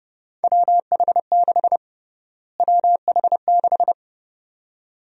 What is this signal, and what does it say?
Morse code, heard as a keyed tone, about 32 wpm, 700 Hz. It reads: WH6 WH6
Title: USA Call Sign Prefixes - Repeated 2x - 30wpm